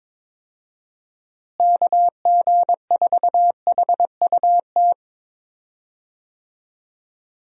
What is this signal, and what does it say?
Morse code, heard as a keyed tone, about 22 wpm, 700 Hz. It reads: KG4HUT
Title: USA Call Signs - 6 Characters - Repeated 3x - 22wpm